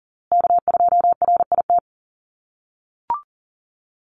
Morse code, heard as a keyed tone, about 40 wpm, 700 Hz. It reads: K2RIT
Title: USA Call Signs Repeated 2x 40wpm